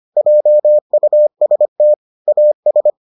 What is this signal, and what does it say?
Morse code, heard as a keyed tone, about 25 wpm, 600 Hz. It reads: JUST AS